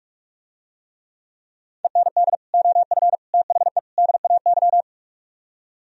Morse code, heard as a keyed tone, about 46 wpm, 700 Hz. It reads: END OF THE DAY